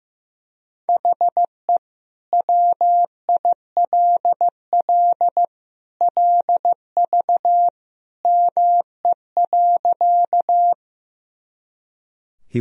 Morse code, heard as a keyed tone, about 15 wpm, 700 Hz. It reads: HE WILL LV ME.
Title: Sentences from Top 400 Words and Abbreviations 15wpm